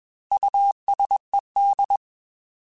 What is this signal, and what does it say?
Morse code, heard as a keyed tone, about 21 wpm, 800 Hz. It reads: USED
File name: Technical Subjects / Technical Subjects-mp3s / Reactance @20